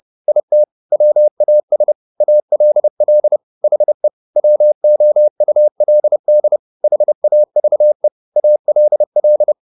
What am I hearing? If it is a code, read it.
Morse code, heard as a keyed tone, about 30 wpm, 600 Hz. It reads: IT WAS ALL HE WOULD HAVE ALL
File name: the old man and the sea / the old man and the sea 1 30wpm